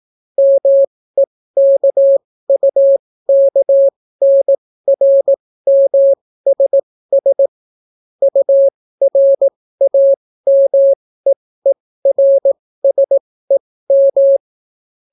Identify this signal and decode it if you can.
Morse code, heard as a keyed tone, about 18 wpm, spaced as for 14 wpm, 550 Hz. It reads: MEKUKNRMSS URAMEERSEM